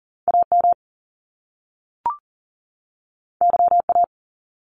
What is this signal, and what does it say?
Morse code, heard as a keyed tone, about 38 wpm, 700 Hz. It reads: AM YA
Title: Top Two Letter Combinations in Words 40wpm